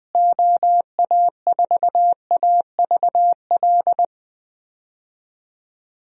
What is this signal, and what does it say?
Morse code, heard as a keyed tone, about 20 wpm, 700 Hz. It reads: OA4AVL
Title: USA Call Signs - 6 Characters - Repeated 3x - 20wpm